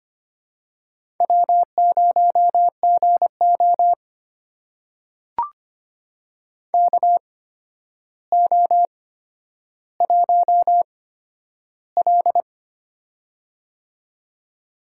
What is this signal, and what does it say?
Morse code, heard as a keyed tone, about 25 wpm, spaced as for 11 wpm, 700 Hz. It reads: W0GO KO1L
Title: USA Call Signs - 4 Characters - 8x Character Spacing - 25wpm